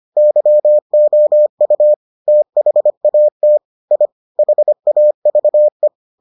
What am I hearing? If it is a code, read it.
Morse code, heard as a keyed tone, about 25 wpm, 600 Hz. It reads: YOU THAT I HAVE